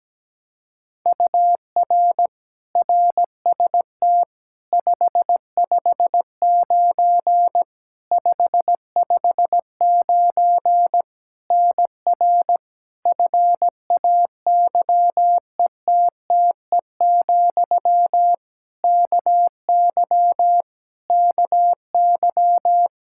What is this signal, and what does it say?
Morse code, heard as a keyed tone, about 17 wpm, 700 Hz. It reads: UR RST 559 559 NR FAYETTE, KY KY
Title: Practice QSO - Exchange 1 - Beginning - 17wpm